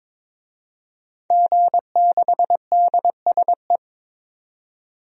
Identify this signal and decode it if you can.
Morse code, heard as a keyed tone, about 22 wpm, 700 Hz. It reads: G6DSE
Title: DXCC-100 Call Signs - 5 Characters - 22wpm